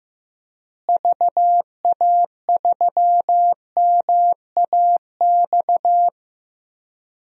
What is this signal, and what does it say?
Morse code, heard as a keyed tone, about 15 wpm, 700 Hz. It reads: VA3MAX
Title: DXCC-100 Call Signs - 6 Characters - Repeated 2x - 15wpm